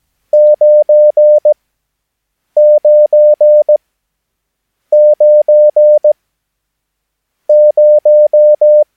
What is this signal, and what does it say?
Morse code, heard as a keyed tone, about 17 wpm, spaced as for 7 wpm, 600 Hz. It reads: 999 0
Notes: Övningsmaterial med alfabetet A till Ö samt siffror 0 till 9 enligt International/Continental Morse Code.